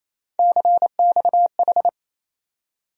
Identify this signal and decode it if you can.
Morse code, heard as a keyed tone, about 28 wpm, 700 Hz. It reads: CXH
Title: USA Call Sign Postfixes - 3 Letters 28wpm